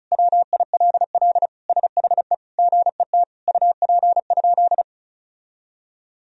Morse code, heard as a keyed tone, about 36 wpm, 700 Hz. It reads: WILL SHE GET UP?
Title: Sentences from Top 1000 Words - Review Part 1 - 35wpm